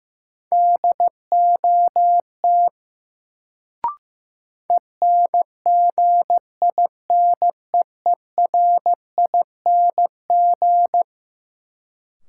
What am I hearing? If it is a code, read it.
Morse code, heard as a keyed tone, about 15 wpm, 700 Hz. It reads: DOT ENGINEERING
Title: Sentences from Top 1500 Words - New Word Drill Alt - 15wpm